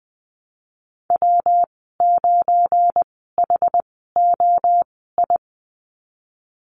Morse code, heard as a keyed tone, about 20 wpm, spaced as for 15 wpm, 700 Hz. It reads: W9HOI
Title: 179 Lesson 29g - Slash - Call Signs 20wpm